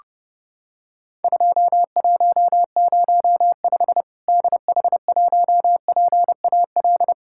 Morse code, heard as a keyed tone, about 30 wpm, 700 Hz. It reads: 2105 DH1PAL
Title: Worked All Europe DX Contest - QTC Exchange - 30wpm